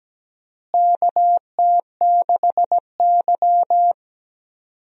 Morse code, heard as a keyed tone, about 17 wpm, 700 Hz. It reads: KT6Y
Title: USA Call Signs - 4 Characters - Repeated 3x - 17wpm